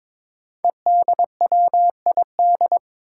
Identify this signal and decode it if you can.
Morse code, heard as a keyed tone, about 21 wpm, 700 Hz. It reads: EDWID
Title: Five Challenging Letter-Number Combinations 22wpm